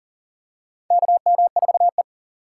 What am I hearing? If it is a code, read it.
Morse code, heard as a keyed tone, about 40 wpm, 700 Hz. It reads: KM4E
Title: USA Call Signs - 3x Character Spacing - 40wpm